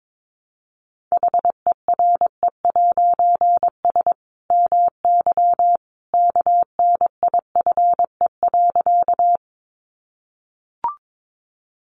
Morse code, heard as a keyed tone, about 22 wpm, 700 Hz. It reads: HERE'S MY KNIFE.
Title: A Sound Of Thunder 22wpm